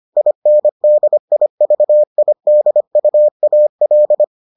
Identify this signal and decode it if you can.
Morse code, heard as a keyed tone, about 25 wpm, 600 Hz. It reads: INDIVIDUAL